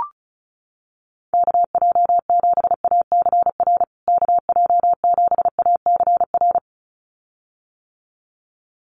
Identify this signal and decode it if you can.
Morse code, heard as a keyed tone, about 34 wpm, 700 Hz. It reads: KJ7ACR KJ7ACR
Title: USA Call Signs Repeated 2x 35wpm